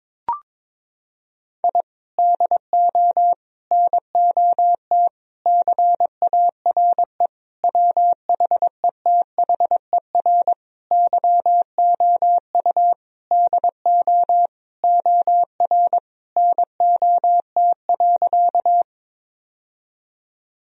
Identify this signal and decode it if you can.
Morse code, heard as a keyed tone, about 22 wpm, 700 Hz. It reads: I DO NOT CARE WHETHER YOU DO OR NOT.
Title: Sentences from Top 1000 Words - Review Part 3 - 22wpm